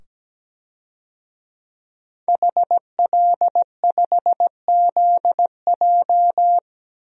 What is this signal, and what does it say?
Morse code, heard as a keyed tone, about 17 wpm, 700 Hz. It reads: HL5ZJ
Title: DXCC-100 Call Signs - 5 Characters - 17wpm